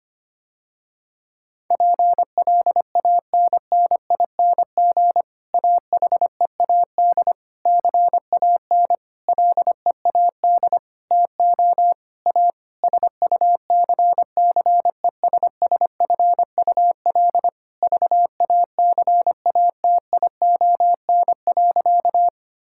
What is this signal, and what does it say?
Morse code, heard as a keyed tone, about 25 wpm, 700 Hz. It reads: PLANNING AHEAD CAN LEAD TO A SUCCESSFUL VACATION.
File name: practice / 25wpm / Difficult Sentences with Hints - 25wpm